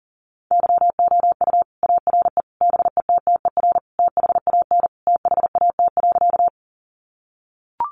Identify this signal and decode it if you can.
Morse code, heard as a keyed tone, about 40 wpm, 700 Hz. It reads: YOU ARE BETTER THAN THAT.
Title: Sentences from Top 400 Words 40wpm